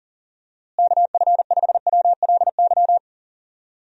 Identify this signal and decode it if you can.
Morse code, heard as a keyed tone, about 40 wpm, 700 Hz. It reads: KF5WLY
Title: USA Call Signs - 6 Characters - Repeated 3x - 40wpm